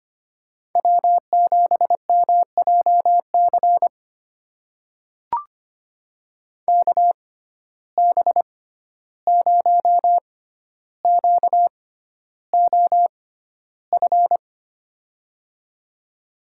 Morse code, heard as a keyed tone, about 25 wpm, spaced as for 12 wpm, 700 Hz. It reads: W7MJC KB0QOF
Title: USA Call Signs - 6x Character Spacing - 25wpm